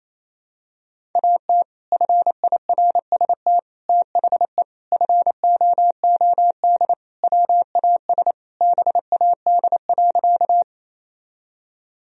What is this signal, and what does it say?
Morse code, heard as a keyed tone, about 28 wpm, 700 Hz. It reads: AT FIRST THE FOOD WAS BAD.
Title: Sentences from Top 1000 Words - Review Part 1 - 28wpm